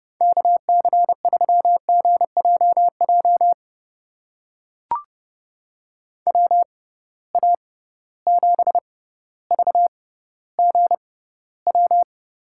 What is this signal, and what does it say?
Morse code, heard as a keyed tone, about 30 wpm, spaced as for 15 wpm, 700 Hz. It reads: KC3GJJ WA7VGW
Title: USA Call Signs - 6 Characters - 6x Character Spacing - 30wpm